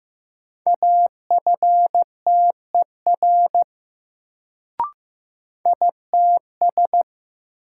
Morse code, heard as a keyed tone, about 16 wpm, 700 Hz. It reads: AFTER ITS
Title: Sets of 2 Words - New Word Drill - 15wpm